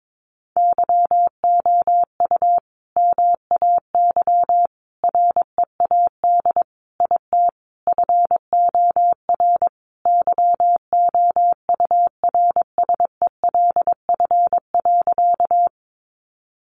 Morse code, heard as a keyed tone, about 22 wpm, 700 Hz. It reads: YOU MAY READ IT FOR YOURSELF.
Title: Sentences from Top 600 Words 22wpm